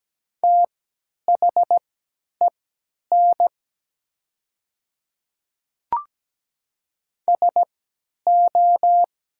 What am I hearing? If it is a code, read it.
Morse code, heard as a keyed tone, about 17 wpm, spaced as for 10 wpm, 700 Hz. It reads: THEN SO
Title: 3x Character Spacing - Top 100 Words - 17wpm